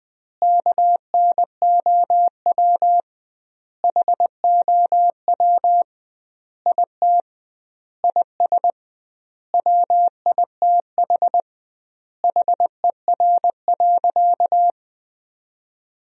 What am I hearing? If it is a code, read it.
Morse code, heard as a keyed tone, about 20 wpm, 700 Hz. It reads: KNOW HOW IT IS WITH HER.
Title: 2x Word-Spacing - Sentences from Top 100 Words - 20wpm